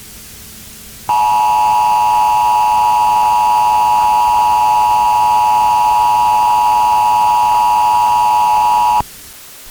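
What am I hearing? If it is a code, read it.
Emergency Alert System (SAME) tones (the 853 and 960 Hz attention tone), with noise. It is heard from 1.0 to 9.0 s.
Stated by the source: Emergency Alert System tone used as a sound in a TF2 (Team Fortress 2) custom map.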